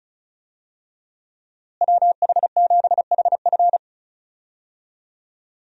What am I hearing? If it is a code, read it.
Morse code, heard as a keyed tone, about 36 wpm, 700 Hz. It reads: WH7HF
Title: USA Call Signs - 5 Characters - Repeated 3x - 35wpm